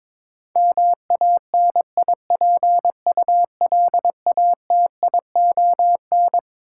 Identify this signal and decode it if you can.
Morse code, heard as a keyed tone, about 22 wpm, 700 Hz. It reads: MANIPULATION
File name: Twelve Letter Words - 22wpm